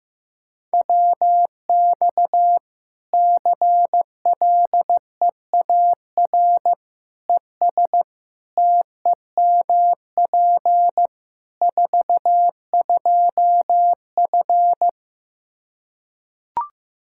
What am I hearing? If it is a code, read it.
Morse code, heard as a keyed tone, about 16 wpm, 700 Hz. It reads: WX CLEAR ES TEMP 42F
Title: Practice QSO - Exchange 2 - Info - 15wpm